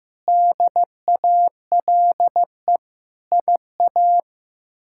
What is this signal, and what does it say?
Morse code, heard as a keyed tone, about 15 wpm, 700 Hz. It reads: DALE IA
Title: K1USN SST QSO Practice - 15wpm